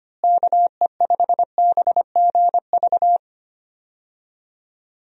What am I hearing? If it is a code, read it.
Morse code, heard as a keyed tone, about 25 wpm, 700 Hz. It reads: KE5BGV
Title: USA Call Signs - 6 Characters - Repeated 2x - 25wpm